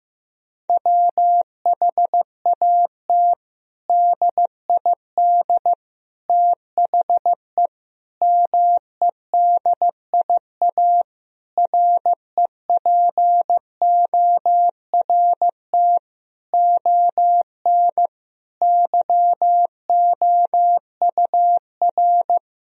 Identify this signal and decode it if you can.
Morse code, heard as a keyed tone, about 15 wpm, 700 Hz. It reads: WHAT DID THE MEDIA REPORT ON YOUR
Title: Sentences from Top 1000 Words - Review Part 4 - 15wpm